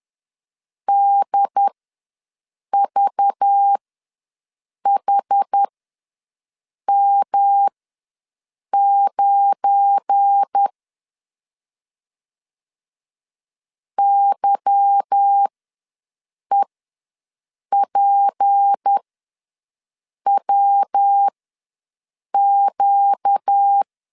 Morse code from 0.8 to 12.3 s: DVHM9